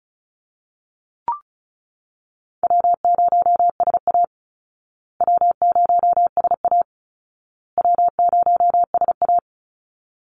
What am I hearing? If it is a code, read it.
Morse code, heard as a keyed tone, about 34 wpm, 700 Hz. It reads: W0SA W0SA W0SA
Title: Easy Call Signs Repeated 3x - 35wpm